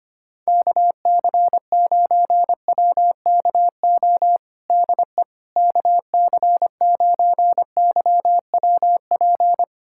Morse code, heard as a keyed tone, about 25 wpm, 700 Hz. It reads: KC9WKO DE KC9YWP
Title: USA Callsign DE Callsign 25wpm